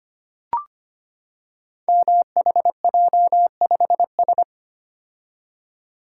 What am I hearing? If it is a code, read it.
Morse code, heard as a keyed tone, about 25 wpm, 700 Hz. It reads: MHJ5S
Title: Code Groups - 5 Letter:Numbers - 25wpm